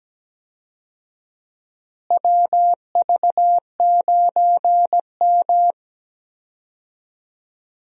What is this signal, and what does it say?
Morse code, heard as a keyed tone, about 17 wpm, 700 Hz. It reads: WV9M
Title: USA Call Signs - 4 Characters - Repeated 2x - 17wpm